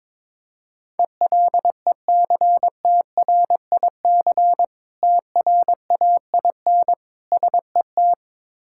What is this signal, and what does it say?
Morse code, heard as a keyed tone, about 22 wpm, 700 Hz. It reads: ELECTRIC TRAIN SET
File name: Difficult Words with Hints - 22wpm